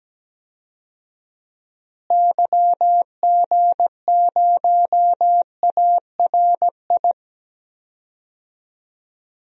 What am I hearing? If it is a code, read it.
Morse code, heard as a keyed tone, about 17 wpm, 700 Hz. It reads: YG0ARI
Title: DXCC-100 Call Signs - 6 Characters - Repeated 3x - 17wpm